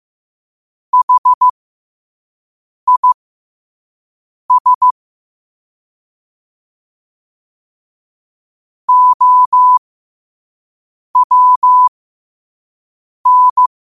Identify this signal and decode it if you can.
Morse code, heard as a keyed tone, about 15 wpm, spaced as for 5 wpm, 1000 Hz. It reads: HIS OWN
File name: qotdfiles / 5wpm